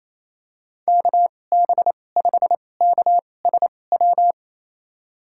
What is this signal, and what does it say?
Morse code, heard as a keyed tone, about 28 wpm, spaced as for 20 wpm, 700 Hz. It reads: KB5KSW
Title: USA Call Signs - 6 Characters - 2x Character Spacing - 28wpm